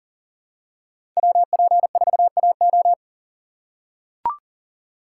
Morse code, heard as a keyed tone, about 39 wpm, 700 Hz. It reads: WP4AO
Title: USA Call Signs - 5 Characters - 3x Character Spacing - 40wpm